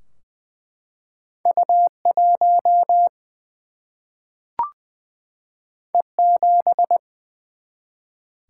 Morse code, heard as a keyed tone, about 21 wpm, 700 Hz. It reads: U 1 E 7
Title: Single-Letter Single-Number 20wpm